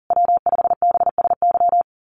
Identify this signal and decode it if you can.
Morse code, heard as a keyed tone, about 40 wpm, 700 Hz. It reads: W5BSY
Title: USA Call Signs Repeated 2x 40wpm